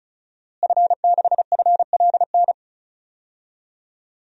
Morse code, heard as a keyed tone, about 36 wpm, 700 Hz. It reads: F6FLN
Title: DXCC-100 Call Signs - 5 Characters - Repeated 3x - 35wpm